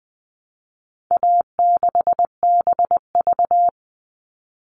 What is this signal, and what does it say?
Morse code, heard as a keyed tone, about 20 wpm, 700 Hz. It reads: A6BV
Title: All Cut Numbers - 4-Digit 20wpm